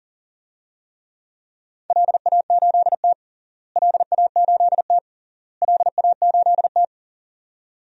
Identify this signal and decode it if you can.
Morse code, heard as a keyed tone, about 37 wpm, 700 Hz. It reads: LA8T LA8T LA8T
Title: DXCC-100 Call Signs - 4 Characters - Repeated 3x - 40wpm